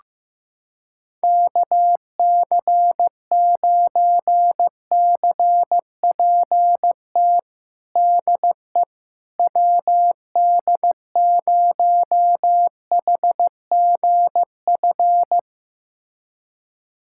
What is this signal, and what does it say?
Morse code, heard as a keyed tone, about 15 wpm, 700 Hz. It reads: KC9CPT DE WD0HGF
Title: USA Callsign DE Callsign - 6 Characters - 15wpm